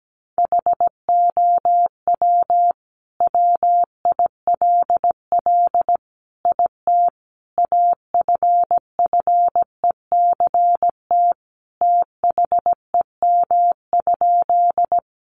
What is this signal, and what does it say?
Morse code, heard as a keyed tone, about 17 wpm, 700 Hz. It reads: HOW WILL IT AFFECT THEM?
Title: Sentences from Top 900 Words 17wpm